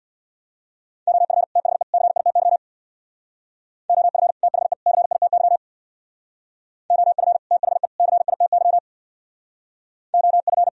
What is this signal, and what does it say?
Morse code, heard as a keyed tone, about 64 wpm, 700 Hz. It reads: OF THE CITY OF THE CITY OF THE CITY OF